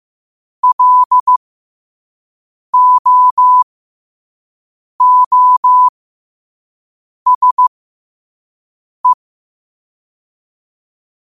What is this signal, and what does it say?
Morse code, heard as a keyed tone, about 15 wpm, spaced as for 5 wpm, 1000 Hz. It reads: LOOSE
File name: qotdfiles / 5wpm